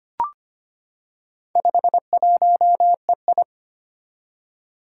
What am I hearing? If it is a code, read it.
Morse code, heard as a keyed tone, about 25 wpm, 700 Hz. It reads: 51EI
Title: Four Challenging Letter-Number Combinations 25wpm